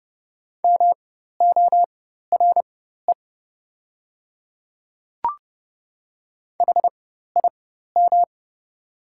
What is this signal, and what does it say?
Morse code, heard as a keyed tone, about 30 wpm, spaced as for 14 wpm, 700 Hz. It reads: MORE HIM
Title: 4x Character Spacing - Top 100 Words - 30wpm